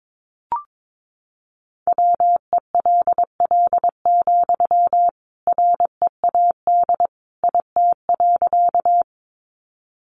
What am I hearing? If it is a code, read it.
Morse code, heard as a keyed tone, about 22 wpm, 700 Hz. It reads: WELL, READ IT.
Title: Sentences from Top 200 Words 22wpm